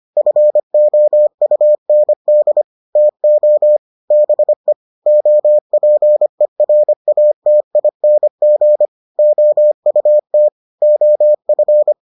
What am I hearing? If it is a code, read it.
Morse code, heard as a keyed tone, about 25 wpm, 600 Hz. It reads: FOUND TO BE OPERATING OUT OF